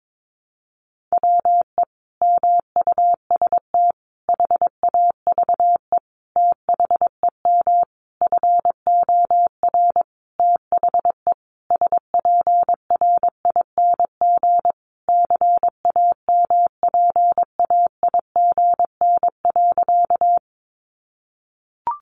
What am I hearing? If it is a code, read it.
Morse code, heard as a keyed tone, about 22 wpm, 700 Hz. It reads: WE MUST HAVE THEM FOR THE SPRING CAMPAIGN.
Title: Sentences from Top 1200 Words 22wpm